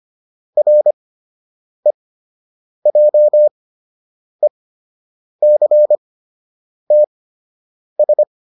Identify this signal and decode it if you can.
Morse code, heard as a keyed tone, about 25 wpm, spaced as for 8 wpm, 600 Hz. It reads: REJECTS